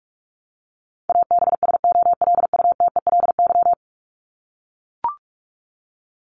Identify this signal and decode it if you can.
Morse code, heard as a keyed tone, about 44 wpm, 700 Hz. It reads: ABSOLUTELY
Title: Top 1501-1750 Words 45wpm